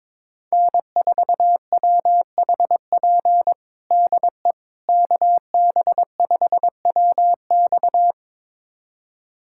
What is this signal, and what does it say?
Morse code, heard as a keyed tone, about 21 wpm, 700 Hz. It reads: N4WHP DE KB5WX
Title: USA Callsign DE Callsign - 5 Characters - 22wpm